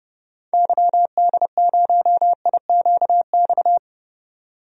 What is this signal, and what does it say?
Morse code, heard as a keyed tone, about 30 wpm, 700 Hz. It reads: YD0IQX
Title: DXCC-100 Call Signs - 6 Characters - 30wpm